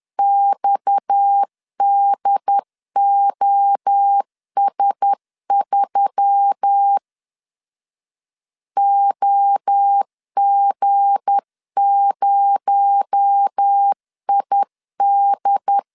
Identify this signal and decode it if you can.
Morse code, heard as a keyed tone, about 10 wpm, 800 Hz. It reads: XDOS3 OG0ID
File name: LEZIONI / 7 GF9Z / 7 G 10wpm 2501